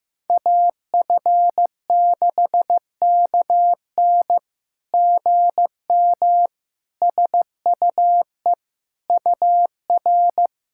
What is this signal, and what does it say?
Morse code, heard as a keyed tone, about 15 wpm, 700 Hz. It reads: AF6KN GM SUE UR
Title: SOTA QSO Practice 15wpm